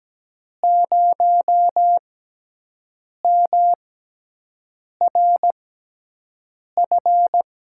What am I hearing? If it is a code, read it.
Morse code, heard as a keyed tone, about 17 wpm, spaced as for 6 wpm, 700 Hz. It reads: 0MRF
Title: USA Call Signs - 6 Characters - 6x Character Spacing - 17wpm